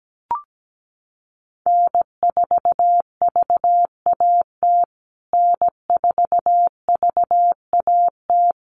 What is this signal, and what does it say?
Morse code, heard as a keyed tone, about 17 wpm, 700 Hz. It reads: N4VAT N4VAT
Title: USA Call Signs Repeated 3x 17wpm